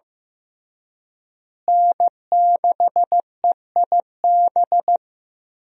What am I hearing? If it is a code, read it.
Morse code, heard as a keyed tone, about 15 wpm, 700 Hz. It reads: N6EIB
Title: USA Call Signs - 5 Characters - Repeated 3x - 15wpm